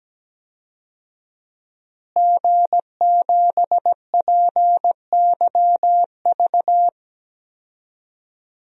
Morse code, heard as a keyed tone, about 17 wpm, 700 Hz. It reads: G7PYV
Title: DXCC-100 Call Signs - 5 Characters - Repeated 2x - 17wpm